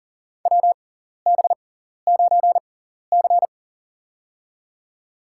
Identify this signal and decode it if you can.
Morse code, heard as a keyed tone, about 40 wpm, spaced as for 14 wpm, 700 Hz. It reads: WB9C
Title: USA Call Signs - 4 Characters - 6x Character Spacing - 40wpm